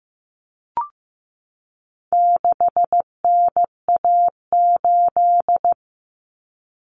Morse code, heard as a keyed tone, about 15 wpm, 700 Hz. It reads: 6NA8
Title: Common Cut Numbers - 4-Digit 15wpm